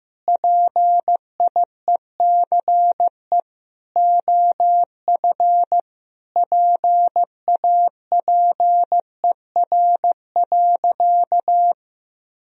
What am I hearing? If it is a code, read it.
Morse code, heard as a keyed tone, about 15 wpm, 700 Hz. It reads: PIECE OF PAPER.
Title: Sentences from Top 1000 Words - Review Part 2 - 15wpm